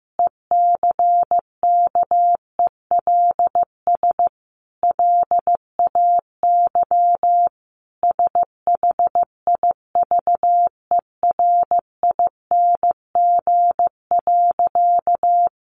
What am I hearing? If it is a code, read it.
Morse code, heard as a keyed tone, about 15 wpm, 700 Hz. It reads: ECKELS LAY SHIVERING.
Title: A Sound Of Thunder 15wpm